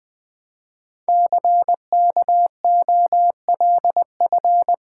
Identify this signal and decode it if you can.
Morse code, heard as a keyed tone, about 20 wpm, 700 Hz. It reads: CKOLF
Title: Code Groups - 5 Letters - 20wpm